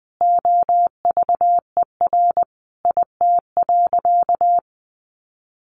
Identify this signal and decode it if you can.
Morse code, heard as a keyed tone, about 19 wpm, 700 Hz. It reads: OVER IT.
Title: Sentences from Top 1500 Words 20wpm